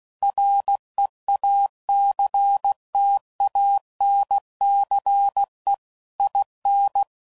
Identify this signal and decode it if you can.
Morse code, heard as a keyed tone, about 16 wpm, 800 Hz. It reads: REACTANCE IN
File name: CW Code Groups @15WPM / Technical Subjects / XC